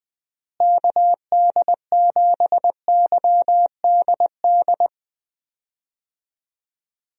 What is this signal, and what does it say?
Morse code, heard as a keyed tone, about 20 wpm, 700 Hz. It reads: KD7YDD
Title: USA Call Signs - 6 Characters - Repeated 2x - 20wpm